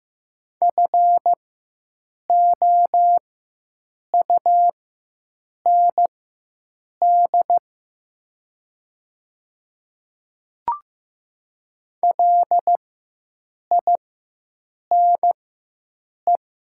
Morse code, heard as a keyed tone, about 15 wpm, spaced as for 7 wpm, 700 Hz. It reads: FOUND LINE
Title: 4x Character Spacing - Top 300 Words - 15wpm